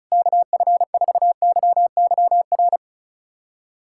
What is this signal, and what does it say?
Morse code, heard as a keyed tone, about 34 wpm, 690 Hz. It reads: KF4YYR
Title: USA Callsign DE Callsign 35wpm